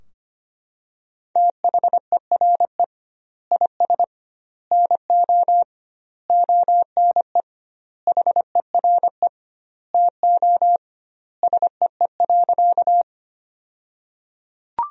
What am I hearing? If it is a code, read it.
Morse code, heard as a keyed tone, about 25 wpm, 700 Hz. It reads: THERE IS NO ONE HERE TO SEE.
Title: 2x Word-Spacing - Sentences from Top 200 Words - 25wpm